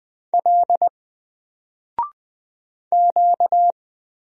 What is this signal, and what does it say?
Morse code, heard as a keyed tone, about 20 wpm, 700 Hz. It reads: L Q
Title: Single Character (Letter:Number:Punctuation:Prosign) 20wpm